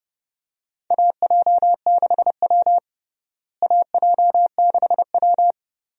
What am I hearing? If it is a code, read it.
Morse code, heard as a keyed tone, about 29 wpm, 700 Hz. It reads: AJ6W AJ6W
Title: USA Call Signs - 4 Characters - Repeated 2x - 30wpm